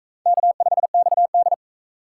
Morse code, heard as a keyed tone, about 38 wpm, 700 Hz. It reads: K5XD